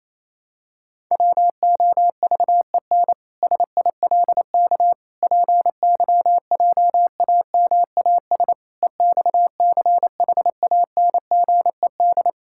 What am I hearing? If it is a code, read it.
Morse code, heard as a keyed tone, about 28 wpm, 700 Hz. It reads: WOVEN SILK PYJAMAS EXCHANGED
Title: Pangram Challenge v2 - 28wpm